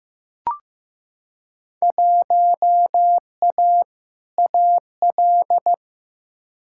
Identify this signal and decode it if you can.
Morse code, heard as a keyed tone, about 15 wpm, 700 Hz. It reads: 1A AL
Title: Field Day Exchanges - 15wpm